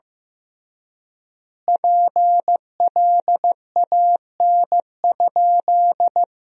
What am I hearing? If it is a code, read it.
Morse code, heard as a keyed tone, about 15 wpm, 700 Hz. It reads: PLAN?
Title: 3x Word-Spacing - Sentences from Top 500 Words - 15wpm